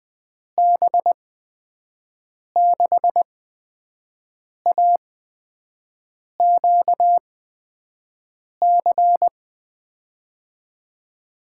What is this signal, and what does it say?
Morse code, heard as a keyed tone, about 20 wpm, spaced as for 5 wpm, 700 Hz. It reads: B6AQC
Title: USA Call Signs - 6 Characters - 8x Character Spacing - 20wpm